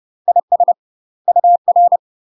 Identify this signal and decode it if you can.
Morse code, heard as a keyed tone, about 30 wpm, 700 Hz. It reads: IS UR
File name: UK MASTER QSOs TRAINING / 30WPM / UK MASTER QSOs 0051